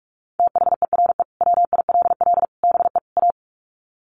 Morse code, heard as a keyed tone, about 44 wpm, 700 Hz. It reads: THERE WILL BE A
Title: Sets of 4 Words 45wpm